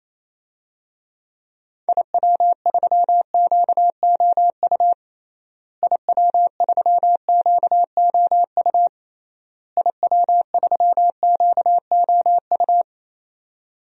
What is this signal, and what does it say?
Morse code, heard as a keyed tone, about 27 wpm, 700 Hz. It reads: IW3QOU IW3QOU IW3QOU
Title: DXCC-100 Call Signs - 6 Characters - Repeated 3x - 28wpm